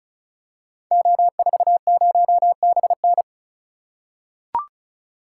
Morse code, heard as a keyed tone, about 36 wpm, 700 Hz. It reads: O40BN
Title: Code Groups - 5 Letter:Numbers - 35wpm